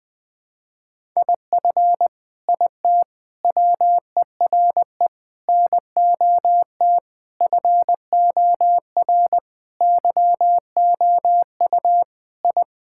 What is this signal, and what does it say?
Morse code, heard as a keyed tone, about 20 wpm, 700 Hz. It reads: IF IT WERE NOT FOR YOU I
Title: Sentences with 5-Word Phrases - 20wpm